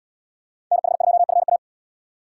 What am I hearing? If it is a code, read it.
Morse code, heard as a keyed tone, about 72 wpm, 700 Hz. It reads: NH2FA